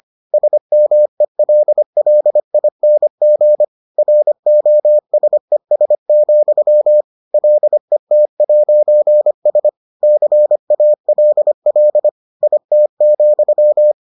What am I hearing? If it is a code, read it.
Morse code, heard as a keyed tone, about 25 wpm, 600 Hz. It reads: SMELLING ROSES, LET'S CALL IT,